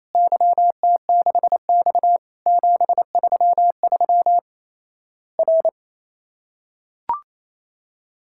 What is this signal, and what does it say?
Morse code, heard as a keyed tone, about 28 wpm, 700 Hz. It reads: YT6X 733 R
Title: Worked All Europe DX Contest - QTC Exchange - 28wpm